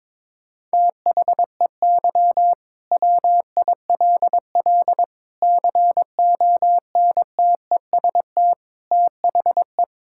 Morse code, heard as a keyed tone, about 22 wpm, 700 Hz. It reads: THEY WILL CONTEST THE
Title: Sentences from Top 1600 Words 22wpm